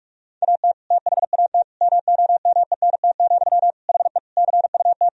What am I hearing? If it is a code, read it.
Morse code, heard as a keyed tone, about 46 wpm, 700 Hz. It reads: AT THAT MOMENT, HE CUT
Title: Sentences from Top 1000 Words - Review Part 1 - 45wpm